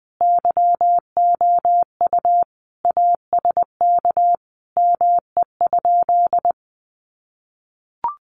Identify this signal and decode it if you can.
Morse code, heard as a keyed tone, about 20 wpm, 700 Hz. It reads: YOU ASK ME?
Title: Sentences from Top 200 Words 20wpm